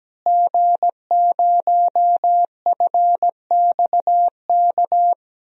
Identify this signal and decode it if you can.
Morse code, heard as a keyed tone, about 17 wpm, 700 Hz. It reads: G0FXK
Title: DXCC-100 Call Signs - 5 Characters - Repeated 2x - 17wpm